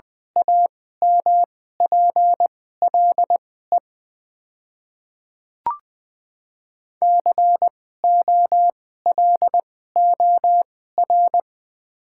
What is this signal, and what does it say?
Morse code, heard as a keyed tone, about 20 wpm, spaced as for 14 wpm, 700 Hz. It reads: AMPLE COLOR
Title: 2x Character Spacing - Top 300 Words - 20wpm